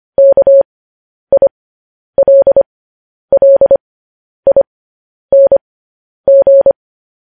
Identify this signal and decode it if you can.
Morse code, heard as a keyed tone, about 25 wpm, spaced as for 10 wpm, 550 Hz. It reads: KILLING
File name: podcast / News headlines-1x-25x10